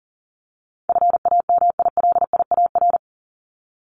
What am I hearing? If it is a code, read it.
Morse code, heard as a keyed tone, about 40 wpm, 700 Hz. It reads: FAMILIAR
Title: Top 751-1000 Words 40wpm